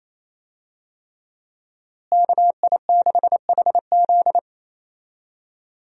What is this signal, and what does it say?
Morse code, heard as a keyed tone, about 28 wpm, 700 Hz. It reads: KI6HZ
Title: USA Call Signs - 5 Characters - Repeated 3x - 28wpm